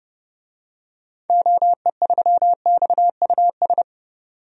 Morse code, heard as a keyed tone, about 29 wpm, 700 Hz. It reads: OE3XUS
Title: DXCC-100 Call Signs - 6 Characters - 30wpm